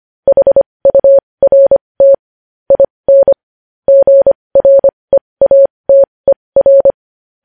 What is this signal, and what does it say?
Morse code, heard as a keyed tone, about 25 wpm, spaced as for 20 wpm, 550 Hz. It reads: HURT IN GREATER
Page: news-headlines-in-morse-code-at-20-wpm